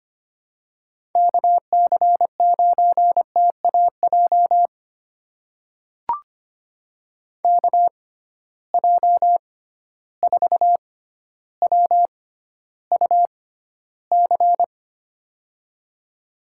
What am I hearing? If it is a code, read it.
Morse code, heard as a keyed tone, about 25 wpm, spaced as for 13 wpm, 700 Hz. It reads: KC9TAJ KJ4WUC
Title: USA Call Signs - 6 Characters - 6x Character Spacing - 25wpm